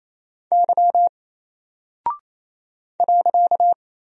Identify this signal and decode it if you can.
Morse code, heard as a keyed tone, about 27 wpm, 700 Hz. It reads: Y .
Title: Single Character (Letter:Number:Punctuation:Prosign) 28wpm